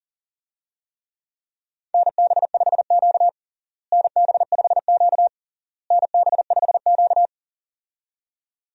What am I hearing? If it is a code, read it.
Morse code, heard as a keyed tone, about 40 wpm, 700 Hz. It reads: NB5Q NB5Q NB5Q
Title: USA Call Signs - 4 Characters - Repeated 3x - 40wpm